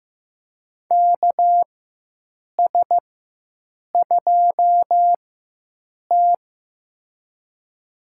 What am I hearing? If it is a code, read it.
Morse code, heard as a keyed tone, about 15 wpm, spaced as for 7 wpm, 700 Hz. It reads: KS2T
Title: USA Call Signs - 4 Characters - 4x Character Spacing - 15wpm